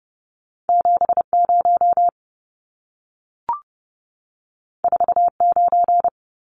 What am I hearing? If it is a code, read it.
Morse code, heard as a keyed tone, about 30 wpm, 700 Hz. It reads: 70 49
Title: Numbers - 2-Digit 30wpm